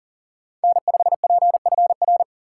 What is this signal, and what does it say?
Morse code, heard as a keyed tone, about 40 wpm, 700 Hz. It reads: N5PFR
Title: USA Call Signs - 2x Character Spacing - 40wpm